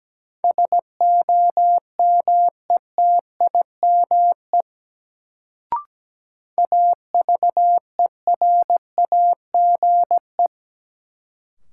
Morse code, heard as a keyed tone, about 17 wpm, 700 Hz. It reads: SOMETIME AVERAGE
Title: Sentences from Top 1000 Words - Review Part 6 - New Word Drill - 17wpm